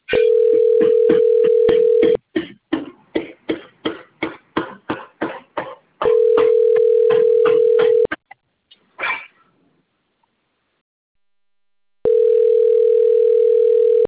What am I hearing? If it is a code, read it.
Ringback tone (North America), 440 + 480 Hz.